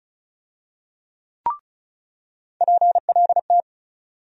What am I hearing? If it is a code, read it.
Morse code, heard as a keyed tone, about 36 wpm, 700 Hz. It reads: PLT
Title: USA Call Sign Postfixes - 3 Letters 35wpm